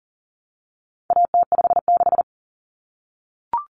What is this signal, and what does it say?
Morse code, heard as a keyed tone, about 40 wpm, 700 Hz. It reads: AT56
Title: Common Cut Numbers - 4-Digit 40wpm